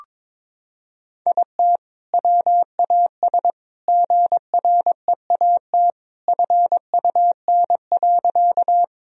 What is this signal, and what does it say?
Morse code, heard as a keyed tone, about 22 wpm, 700 Hz. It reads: IT WAS GREAT FUN.
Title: Sentences from Top 1000 Words - Review Part 4 - 22wpm